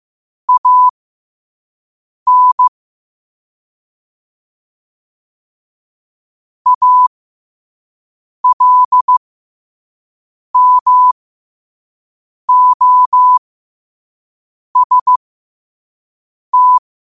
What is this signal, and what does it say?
Morse code, heard as a keyed tone, about 15 wpm, spaced as for 5 wpm, 1000 Hz. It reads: AN ALMOST
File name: qotdfiles / 5wpm